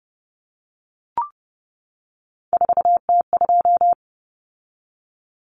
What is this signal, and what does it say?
Morse code, heard as a keyed tone, about 30 wpm, 700 Hz. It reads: 4T2
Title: Common Cut Numbers - 3-Digit 30wpm